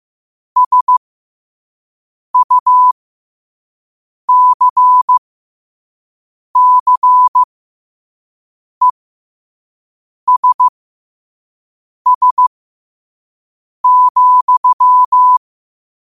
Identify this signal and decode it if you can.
Morse code, heard as a keyed tone, about 15 wpm, spaced as for 5 wpm, 1000 Hz. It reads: SUCCESS,
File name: qotdfiles / 5wpm